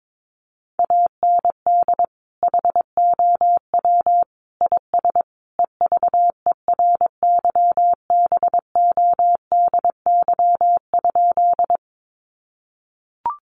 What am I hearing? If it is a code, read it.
Morse code, heard as a keyed tone, about 22 wpm, 700 Hz. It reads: AND HOW IS EVERYBODY?
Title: Sentences from Top 1400 Words 22wpm